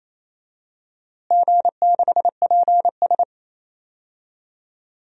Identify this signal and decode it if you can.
Morse code, heard as a keyed tone, about 28 wpm, 700 Hz. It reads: G6PS
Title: DXCC-100 Call Signs - 4 Characters - Repeated 2x - 28wpm